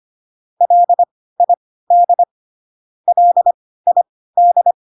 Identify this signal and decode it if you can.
Morse code, heard as a keyed tone, about 25 wpm, spaced as for 16 wpm, 700 Hz. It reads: LID LID